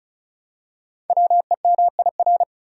Morse code, heard as a keyed tone, about 36 wpm, 700 Hz. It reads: WEMIR
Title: Code Groups - 5 Letters - 35wpm